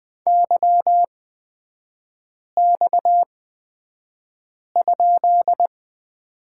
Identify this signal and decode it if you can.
Morse code, heard as a keyed tone, about 20 wpm, spaced as for 5 wpm, 700 Hz. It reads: YX?
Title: 229 Lesson 37e - Letter x - Three Characters 20wpm